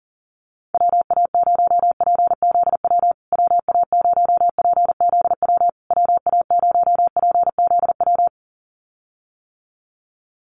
Morse code, heard as a keyed tone, about 40 wpm, 700 Hz. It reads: WA0PZW WA0PZW WA0PZW
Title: USA Call Signs Repeated 3x 40wpm